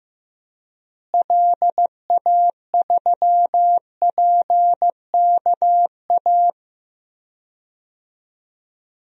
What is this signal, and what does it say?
Morse code, heard as a keyed tone, about 16 wpm, 700 Hz. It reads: LA3PKA
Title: DXCC-100 Call Signs - 6 Characters - Repeated 2x - 15wpm